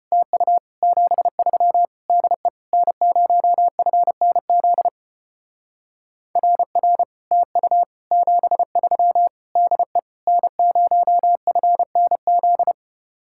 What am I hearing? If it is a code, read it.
Morse code, heard as a keyed tone, about 33 wpm, 700 Hz. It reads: TU 73 DE N0FNZ RR TU 73 DE N0FNZ